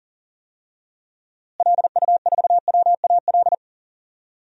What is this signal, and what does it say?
Morse code, heard as a keyed tone, about 40 wpm, 700 Hz. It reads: LU4WAL
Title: DXCC-100 Call Signs - 6 Characters - Repeated 3x - 40wpm